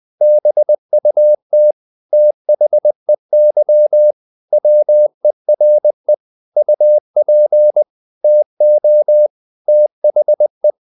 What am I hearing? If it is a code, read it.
Morse code, heard as a keyed tone, about 20 wpm, 600 Hz. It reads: BUT THEY WERE UP TO THE
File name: the old man and the sea part 2 20wpm